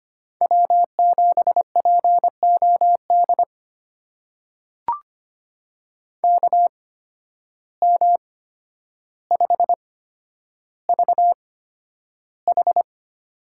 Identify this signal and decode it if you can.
Morse code, heard as a keyed tone, about 25 wpm, spaced as for 11 wpm, 700 Hz. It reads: W7POD KM5VH
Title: USA Call Signs - 5 Characters - 8x Character Spacing - 25wpm